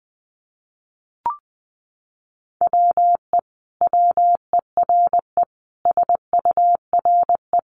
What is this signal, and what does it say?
Morse code, heard as a keyed tone, about 19 wpm, 700 Hz. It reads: WE WERE SURE
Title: Sentences from Top 1100 Words 20wpm